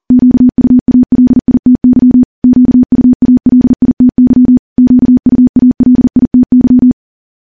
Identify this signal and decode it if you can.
Morse code, heard as a keyed tone, about 40 wpm, 250 Hz. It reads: QUALITY QUALITY QUALITY
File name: morse / morse-training-40WPM